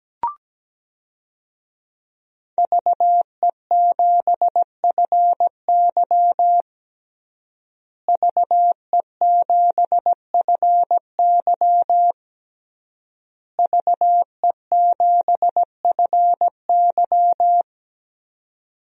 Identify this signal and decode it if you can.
Morse code, heard as a keyed tone, about 17 wpm, 700 Hz. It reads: VE7FY VE7FY VE7FY
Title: DXCC-100 Call Signs - 5 Characters - Repeated 3x - 17wpm